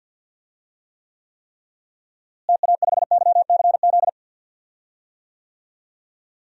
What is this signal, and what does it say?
Morse code, heard as a keyed tone, about 51 wpm, 700 Hz. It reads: TA5YCZ
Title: DXCC-100 Call Signs - 6 Characters - Repeated 3x - 50wpm